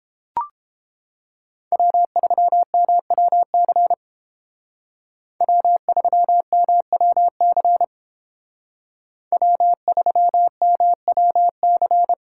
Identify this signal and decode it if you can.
Morse code, heard as a keyed tone, about 30 wpm, 700 Hz. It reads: W3MWC W3MWC W3MWC